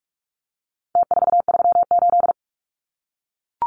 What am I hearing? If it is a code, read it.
Morse code, heard as a keyed tone, about 44 wpm, 700 Hz. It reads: T438
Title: Numbers - Contest Style - 4-Digit 45wpm